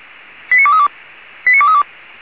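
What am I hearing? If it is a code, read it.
Five-tone selective call (ZVEI), heard as quick runs of five tones, tones of about 70 ms. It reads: 89232 89232
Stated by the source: ZVEI five-tone call received by SDR.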